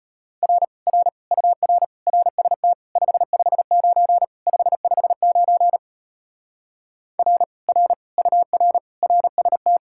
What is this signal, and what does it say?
Morse code, heard as a keyed tone, about 33 wpm, 700 Hz. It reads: R R UR RST 559 559 R R UR RST